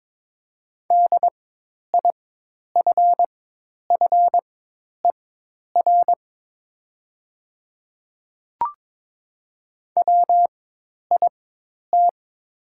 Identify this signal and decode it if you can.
Morse code, heard as a keyed tone, about 22 wpm, spaced as for 10 wpm, 700 Hz. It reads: DIFFER WIT
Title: 4x Character Spacing - Top 200 Words - 22wpm